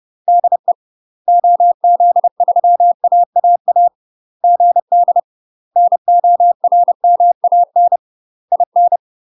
Morse code, heard as a keyed tone, about 30 wpm, 700 Hz. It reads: DE OZ3AAA GD NORMAN IN
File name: UK MASTER QSOs TRAINING / 30WPM / UK MASTER QSOs 0004